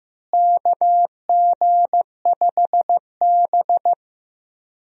Morse code, heard as a keyed tone, about 15 wpm, 700 Hz. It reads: KG5B
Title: USA Call Signs - 4 Characters - Repeated 2x - 15wpm